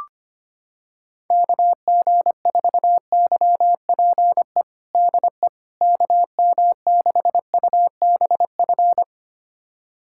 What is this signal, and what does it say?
Morse code, heard as a keyed tone, about 25 wpm, 700 Hz. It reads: KG4YPE DE KM6UBF
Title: USA Callsign DE Callsign 25wpm